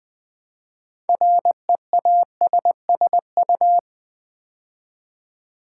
Morse code, heard as a keyed tone, about 20 wpm, 700 Hz. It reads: REASSU
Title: Build A Word - 2 Letters At A Time - 8-Letter Words - 20wpm